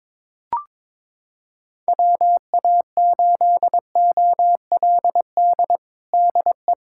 Morse code, heard as a keyed tone, about 21 wpm, 700 Hz. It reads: WA8OLD DE
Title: USA Callsign DE Callsign - 6 Characters - 22wpm